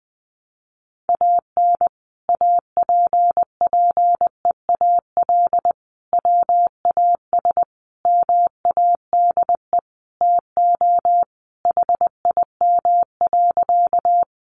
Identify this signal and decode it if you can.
Morse code, heard as a keyed tone, about 20 wpm, 700 Hz. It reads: AN APPEAL WAS MADE TO HIM.
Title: Sentences from Top 1300 Words 20wpm